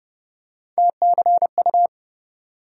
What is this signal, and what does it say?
Morse code, heard as a keyed tone, about 30 wpm, 700 Hz. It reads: TCU
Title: USA Call Sign Postfixes - 3 Letters 30wpm